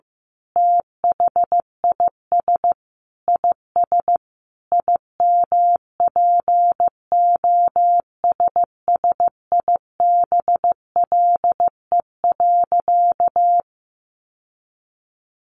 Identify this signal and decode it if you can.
Morse code, heard as a keyed tone, about 15 wpm, 700 Hz. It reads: THIS IS IMPOSSIBLE.
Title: Sentences from Top 1100 Words 15wpm